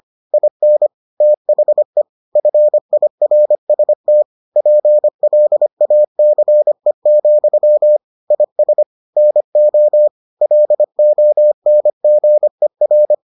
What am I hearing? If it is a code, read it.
Morse code, heard as a keyed tone, about 25 wpm, 600 Hz. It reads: IN THE FIRST PLACE, IS NO LONGER